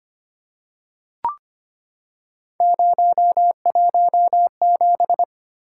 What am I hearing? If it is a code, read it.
Morse code, heard as a keyed tone, about 25 wpm, 700 Hz. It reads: 017
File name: Three Letter-Number Combinations - 25wpm